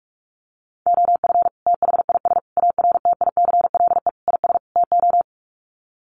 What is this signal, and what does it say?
Morse code, heard as a keyed tone, about 44 wpm, 690 Hz. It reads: OF THIS ARTICLE IS TO
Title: Sets of 5 Words Encore2 45wpm